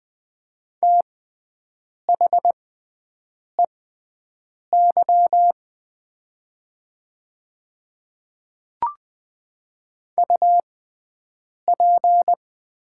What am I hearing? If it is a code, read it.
Morse code, heard as a keyed tone, about 20 wpm, spaced as for 7 wpm, 700 Hz. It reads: THEY UP
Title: 6x Character Spacing - Top 500 Words - 20wpm